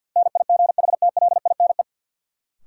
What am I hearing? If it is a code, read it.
Morse code, heard as a keyed tone, about 51 wpm, 700 Hz. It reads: NIGHTLINE
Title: Sets of 5 Words - New Word Drill - 50wpm